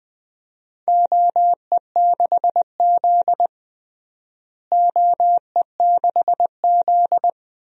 Morse code, heard as a keyed tone, about 20 wpm, 700 Hz. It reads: OE6Z OE6Z
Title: DXCC-100 Call Signs - 4 Characters - Repeated 3x - 20wpm